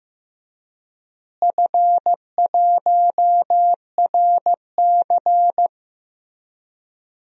Morse code, heard as a keyed tone, about 15 wpm, 700 Hz. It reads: F1RC
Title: DXCC-100 Call Signs - 4 Characters - Repeated 3x - 15wpm